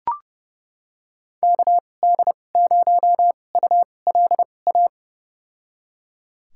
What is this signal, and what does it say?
Morse code, heard as a keyed tone, about 30 wpm, spaced as for 22 wpm, 700 Hz. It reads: KD0ULA
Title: USA Call Signs - 6 Characters - 2x Character Spacing - 30wpm